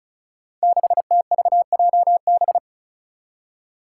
Morse code, heard as a keyed tone, about 36 wpm, 700 Hz. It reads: 6TVJB
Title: Five Challenging Letter-Number Combinations 35wpm